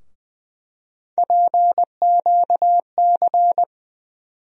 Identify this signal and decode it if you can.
Morse code, heard as a keyed tone, about 20 wpm, 700 Hz. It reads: PQC
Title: USA Call Sign Postfixes - 3 Letters 20wpm